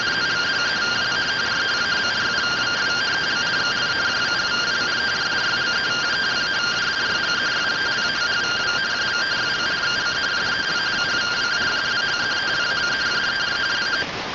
Two-tone teleprinter signal, about 170 Hz shift.